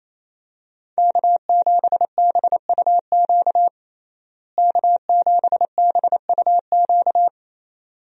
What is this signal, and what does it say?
Morse code, heard as a keyed tone, about 27 wpm, 700 Hz. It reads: K7BUQ K7BUQ
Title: USA Call Signs - 5 Characters - Repeated 2x - 28wpm